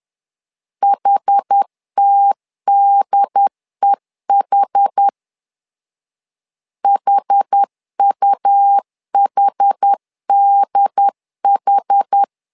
Morse code, heard as a keyed tone, about 10 wpm, 800 Hz. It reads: HTDEH HUHDH
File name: LEZIONI / 2 UHDN / 2 UHD 10wpm 1826